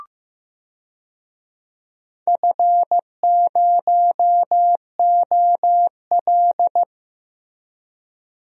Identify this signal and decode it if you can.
Morse code, heard as a keyed tone, about 15 wpm, 700 Hz. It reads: F0OL
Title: DXCC-100 Call Signs - 4 Characters - Repeated 2x - 15wpm